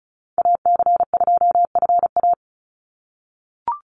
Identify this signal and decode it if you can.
Morse code, heard as a keyed tone, about 34 wpm, 700 Hz. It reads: AC2FA
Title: Call Signs - 35wpm